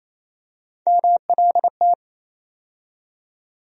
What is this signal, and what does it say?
Morse code, heard as a keyed tone, about 28 wpm, 700 Hz. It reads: MLT
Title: USA Call Sign Postfixes - 3 Letters 28wpm